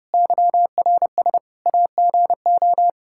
Morse code, heard as a keyed tone, about 30 wpm, 700 Hz. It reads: YRS AGO
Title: Practice QSO - Exchange 3 - More Info - 30wpm